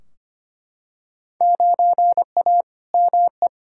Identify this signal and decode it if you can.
Morse code, heard as a keyed tone, about 25 wpm, 700 Hz. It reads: 9A ME
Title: Field Day Exchanges Encore - 25wpm